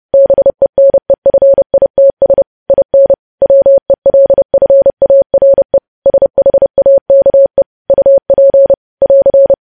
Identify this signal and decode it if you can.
Morse code, heard as a keyed tone, about 30 wpm, 550 Hz. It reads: BENEFITS IN WELFARE SHAKE UP <AR>
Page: news headlines in morse code at 30 wpm